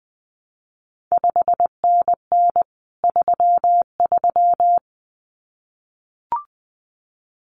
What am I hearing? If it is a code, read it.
Morse code, heard as a keyed tone, about 19 wpm, 700 Hz. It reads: 5NN 33
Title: CQ WW DX Contest Exchange Encore - 20wpm